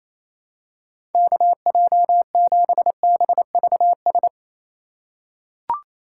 Morse code, heard as a keyed tone, about 28 wpm, 700 Hz. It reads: KJ7BVS
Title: USA Call Signs - 6 Characters - 3x Character Spacing - 28wpm